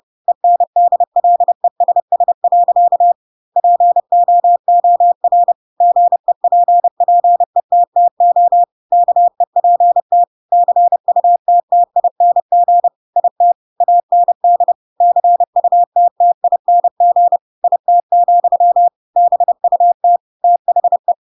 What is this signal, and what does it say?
Morse code, heard as a keyed tone, about 30 wpm, 700 Hz. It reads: ENDLESS. POOR GEPPETTO KEPT CUTTING IT AND CUTTING IT, BUT THE
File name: Pinocchio / Pinocchio ch3 30 30 ik1hge